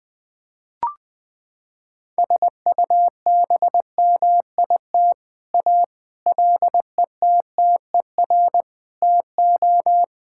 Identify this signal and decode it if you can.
Morse code, heard as a keyed tone, about 20 wpm, 700 Hz. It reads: SUBMIT A LETTER TO
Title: Sets of 6 Words - 20wpm